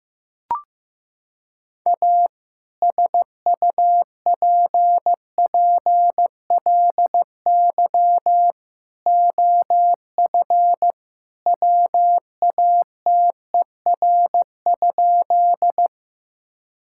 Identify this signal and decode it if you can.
Morse code, heard as a keyed tone, about 15 wpm, 700 Hz. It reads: A SUPPLY OF WATER?
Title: Sentences from Top 1000 Words - Review Part 5 - 15wpm